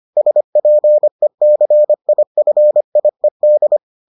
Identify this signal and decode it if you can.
Morse code, heard as a keyed tone, about 25 wpm, 600 Hz. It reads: SPECIFIED